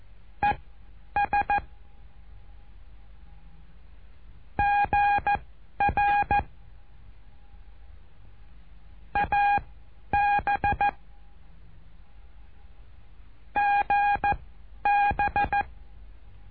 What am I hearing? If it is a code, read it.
Morse code, heard as a keyed tone, about 14 wpm, spaced as for 10 wpm, 820 Hz. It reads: ES GR AB GB
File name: cursos / CURSO DE TELEGRAFIA / Telegrafia - Trilha 08